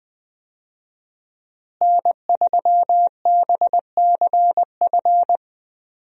Morse code, heard as a keyed tone, about 20 wpm, 700 Hz. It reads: N3BCF
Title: USA Call Signs - 5 Characters - Repeated 3x - 20wpm